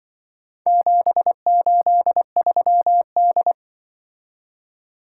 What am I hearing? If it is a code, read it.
Morse code, heard as a keyed tone, about 25 wpm, 700 Hz. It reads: 783D
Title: Speed-Racing - Four Letter-Number Combinations - 20wpm